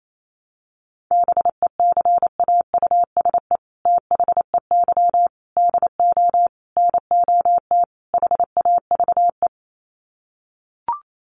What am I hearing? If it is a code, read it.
Morse code, heard as a keyed tone, about 28 wpm, 700 Hz. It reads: BECAUSE THEY DO NOT HAVE
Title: Sets of 5 Words Encore2 28wpm